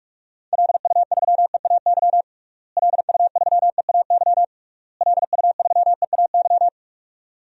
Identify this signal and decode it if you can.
Morse code, heard as a keyed tone, about 46 wpm, 700 Hz. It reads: LU3EAY LU3EAY LU3EAY
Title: DXCC-100 Call Signs - 6 Characters - Repeated 3x - 45wpm